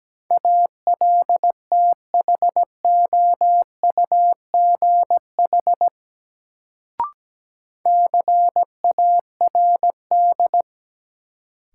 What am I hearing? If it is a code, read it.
Morse code, heard as a keyed tone, about 17 wpm, 700 Hz. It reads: ALTHOUGH CARD
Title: Sentences from Top 600 Words - New Word Drill - 17wpm